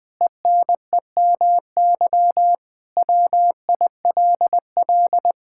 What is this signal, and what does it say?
Morse code, heard as a keyed tone, about 20 wpm, 700 Hz. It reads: ENEMY WILL
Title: Pangram Challenge v2 - 20wpm